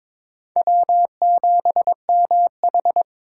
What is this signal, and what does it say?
Morse code, heard as a keyed tone, about 22 wpm, 700 Hz. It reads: W7MH
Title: USA Call Signs - 4 Characters - 6x Character Spacing - 22wpm